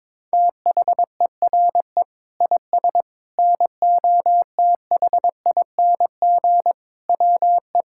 Morse code, heard as a keyed tone, about 21 wpm, 700 Hz. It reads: THERE IS NOTHING WE
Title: Sentences from Top 1000 Words - Review Part 6 - 22wpm